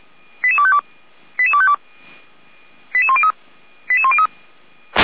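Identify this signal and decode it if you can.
Five-tone selective call (ZVEI), heard as quick runs of five tones, tones of about 70 ms. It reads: 88252 88252 88103 88103
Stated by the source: Received on SDR as ZVEI five-tone selective call 88447, from audio.f85360.feuerwehr-orlamuende.org.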